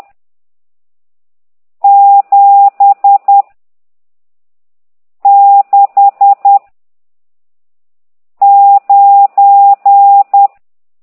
Morse code, heard as a keyed tone, about 10 wpm, 800 Hz. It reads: 7 6 9
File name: ct2idw / exerc14